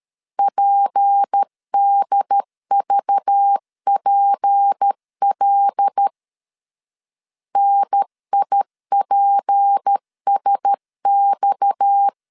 Morse code, heard as a keyed tone, about 13 wpm, 800 Hz. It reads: PDVPL NIPSX
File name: CW / DET / 1 6 1500C S C 12wpm 3533